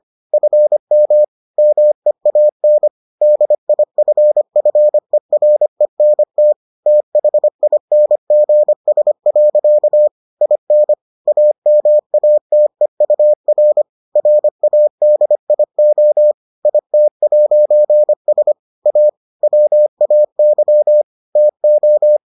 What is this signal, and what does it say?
Morse code, heard as a keyed tone, about 25 wpm, 600 Hz. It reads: FM MEAN DIFFERENT THINGS. IN AMATEUR RADIO IT'S A WAY TO